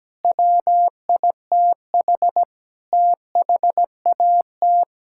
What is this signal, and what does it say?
Morse code, heard as a keyed tone, about 17 wpm, 700 Hz. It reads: WITH THAT
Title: Sentences from Top 1000 Words - Review Part 5 - 17wpm